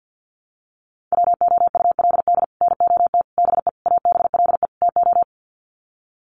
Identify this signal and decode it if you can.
Morse code, heard as a keyed tone, about 50 wpm, 700 Hz. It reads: WOULD NOT BE ABLE TO
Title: Sets of 5 Words 50wpm